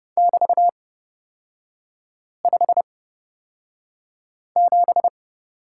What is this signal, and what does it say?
Morse code, heard as a keyed tone, about 31 wpm, spaced as for 5 wpm, 690 Hz. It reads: <BT>57
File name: course / 30:10wpm / 236 Lesson 38e - Letter q - Three Characters 30wpm